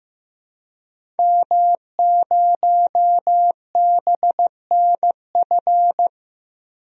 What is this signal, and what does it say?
Morse code, heard as a keyed tone, about 15 wpm, 700 Hz. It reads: M0BNF
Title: DXCC-100 Call Signs - 5 Characters - Repeated 2x - 15wpm